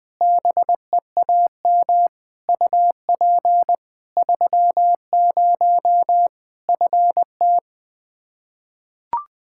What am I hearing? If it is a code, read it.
Morse code, heard as a keyed tone, about 20 wpm, 700 Hz. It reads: BEAM UP 30 FT
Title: Practice QSO - Exchange 2 - Info - 20wpm